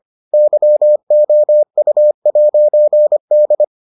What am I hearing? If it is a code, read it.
Morse code, heard as a keyed tone, about 25 wpm, 600 Hz. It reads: YOU'D